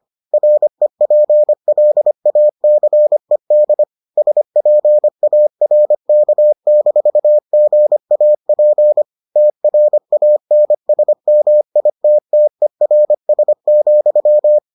Morse code, heard as a keyed tone, about 25 wpm, 600 Hz. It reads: REPLACED SPARK-GAP TRANSMITTERS,